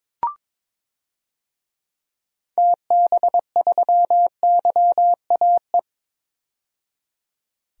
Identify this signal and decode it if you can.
Morse code, heard as a keyed tone, about 22 wpm, 700 Hz. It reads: TB3YAE
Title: DXCC-100 Call Signs - 6 Characters - 22wpm